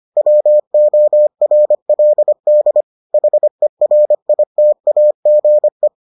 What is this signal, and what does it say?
Morse code, heard as a keyed tone, about 25 wpm, 600 Hz. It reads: WORLD HERITAGE